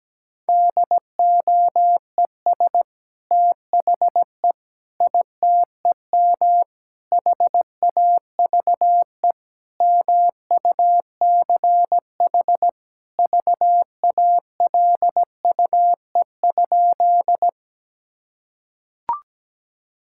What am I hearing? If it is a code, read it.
Morse code, heard as a keyed tone, about 17 wpm, 700 Hz. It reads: DOES THE ITEM HAVE MUCH VALUE?
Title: Sentences from Top 1000 Words - Review Part 6 - 17wpm